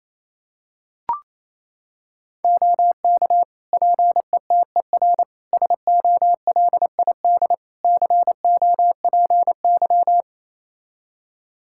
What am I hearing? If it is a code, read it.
Morse code, heard as a keyed tone, about 28 wpm, 700 Hz. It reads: OK PETER SOLID COPY
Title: K1USN SST QSO Practice 28wpm